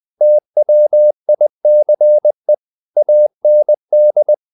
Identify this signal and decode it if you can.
Morse code, heard as a keyed tone, about 20 wpm, 600 Hz. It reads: TWICE AND
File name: the old man and the sea part 2 20wpm